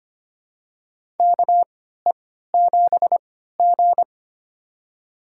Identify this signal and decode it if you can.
Morse code, heard as a keyed tone, about 25 wpm, spaced as for 14 wpm, 700 Hz. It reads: KE7G
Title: USA Call Signs - 4 Characters - 3x Character Spacing - 25wpm